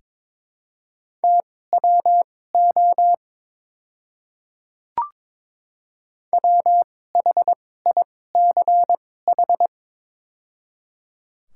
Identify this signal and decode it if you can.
Morse code, heard as a keyed tone, about 22 wpm, spaced as for 16 wpm, 700 Hz. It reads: TWO WHICH
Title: 2x Character Spacing - Top 200 Words - 22wpm